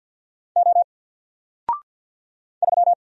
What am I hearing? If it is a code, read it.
Morse code, heard as a keyed tone, about 50 wpm, 700 Hz. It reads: O 3
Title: Single Character (Letter:Number:Punctuation:Prosign) 50wpm